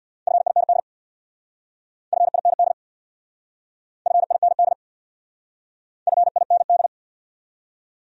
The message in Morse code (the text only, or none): FIND FIND FIND FIND